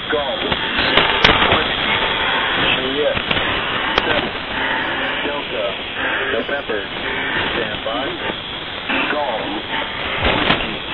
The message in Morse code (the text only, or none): none